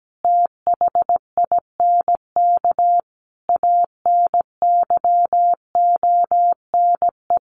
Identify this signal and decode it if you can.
Morse code, heard as a keyed tone, about 17 wpm, 700 Hz. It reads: THINK ANYONE
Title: Sets of 5 Words Encore 17wpm